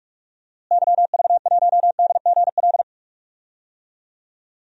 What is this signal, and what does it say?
Morse code, heard as a keyed tone, about 46 wpm, 700 Hz. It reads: YV1DGL
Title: DXCC-100 Call Signs - 6 Characters - Repeated 3x - 45wpm